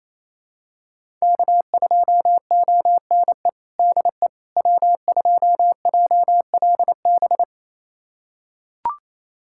Morse code, heard as a keyed tone, about 28 wpm, 700 Hz. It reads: K2ONE DE W2JLB
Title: USA Callsign DE Callsign - 5 Characters - 28wpm